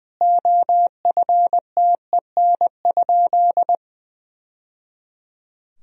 Morse code, heard as a keyed tone, about 20 wpm, 700 Hz. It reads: OFTEN?
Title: Sentences from Top 1000 Words - Review Part 4 - 20wpm